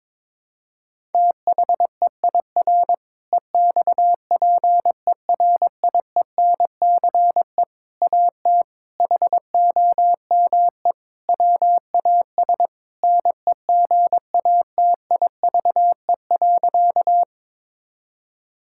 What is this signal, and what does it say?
Morse code, heard as a keyed tone, about 22 wpm, 700 Hz. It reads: THEIR EXPERIENCE AT HOME WAS NEGATIVE.
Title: Sentences from Top 1000 Words - Review Part 6 - 22wpm